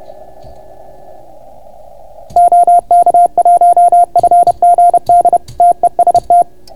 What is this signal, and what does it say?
Morse code, heard as a keyed tone, about 31 wpm, 690 Hz. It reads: OK1FGD TEST